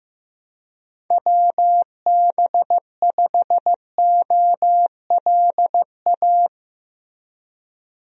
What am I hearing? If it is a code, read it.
Morse code, heard as a keyed tone, about 16 wpm, 700 Hz. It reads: WB5OLA
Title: USA Call Signs - 6 Characters - Repeated 2x - 15wpm